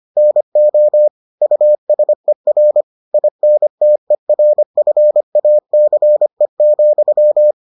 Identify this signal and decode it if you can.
Morse code, heard as a keyed tone, about 25 wpm, 600 Hz. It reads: NO USER INTERFACE,